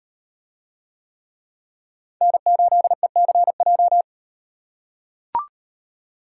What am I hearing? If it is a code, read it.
Morse code, heard as a keyed tone, about 37 wpm, 700 Hz. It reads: N8ECJ